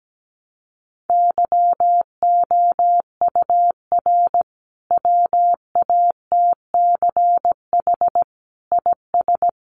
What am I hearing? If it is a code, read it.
Morse code, heard as a keyed tone, about 17 wpm, 700 Hz. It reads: YOUR WATCH IS
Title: Sentences from Top 400 Words 17wpm